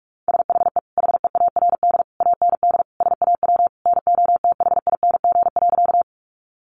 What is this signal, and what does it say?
Morse code, heard as a keyed tone, about 44 wpm, 700 Hz. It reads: SHE HEARD AND SAW NOTHING.
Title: Sentences from Top 500 Words 45wpm